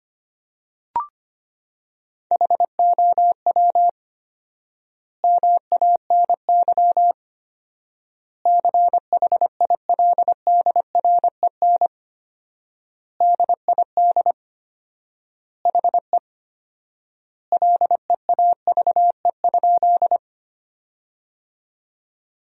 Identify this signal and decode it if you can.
Morse code, heard as a keyed tone, about 25 wpm, 700 Hz. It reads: HOW MANY CHILDREN DID HE LEAVE?
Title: 4x Word-Spacing - Sentences from Top 400 Words - 25wpm